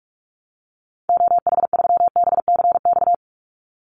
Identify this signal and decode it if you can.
Morse code, heard as a keyed tone, about 44 wpm, 700 Hz. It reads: OH3BCX
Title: Call Signs Repeated 2x - 45wpm